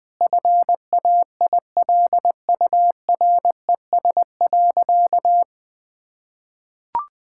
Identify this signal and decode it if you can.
Morse code, heard as a keyed tone, about 20 wpm, 700 Hz. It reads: FAILURES.
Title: Sentences with 7-Word Phrases - Spoken Beginning - 20wpm